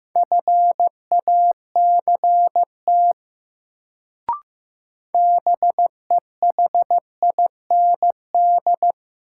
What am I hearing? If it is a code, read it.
Morse code, heard as a keyed tone, about 15 wpm, 700 Hz. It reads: FACT BEHIND
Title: Sentences from Top 500 Words - New Word Drill - 15wpm